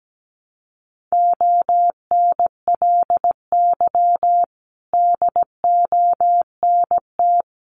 Morse code, heard as a keyed tone, about 17 wpm, 700 Hz. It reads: ONLY DONT
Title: Sentences from Top 900 Words 17wpm